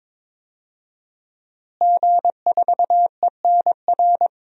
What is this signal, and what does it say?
Morse code, heard as a keyed tone, about 22 wpm, 700 Hz. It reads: G4ENR
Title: DXCC-100 Call Signs - 5 Characters - Repeated 3x - 22wpm